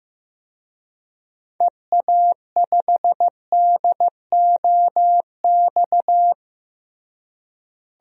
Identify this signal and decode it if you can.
Morse code, heard as a keyed tone, about 15 wpm, 700 Hz. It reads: EA5DOX
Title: DXCC-100 Call Signs - 6 Characters - Repeated 2x - 15wpm